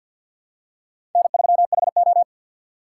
Morse code, heard as a keyed tone, about 50 wpm, 700 Hz. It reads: N3HO
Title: USA Call Signs - 4 Characters - 6x Character Spacing - 50wpm